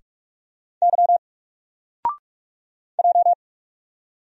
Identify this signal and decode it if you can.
Morse code, heard as a keyed tone, about 46 wpm, 700 Hz. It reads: Y J
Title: Single Letter-Number 45wpm